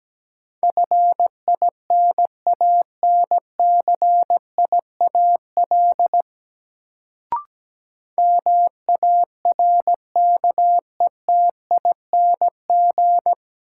Morse code, heard as a keyed tone, about 17 wpm, 700 Hz. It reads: FINANCIAL MARKETING
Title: Sentences from Top 1000 Words - Review Part 5 - New Word Drill - 17wpm